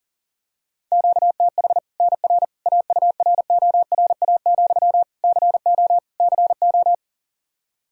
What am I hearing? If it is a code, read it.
Morse code, heard as a keyed tone, about 37 wpm, 700 Hz. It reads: QTH NR AURORA, CO CO
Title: Practice QSO - Exchange 1 - Beginning - 40wpm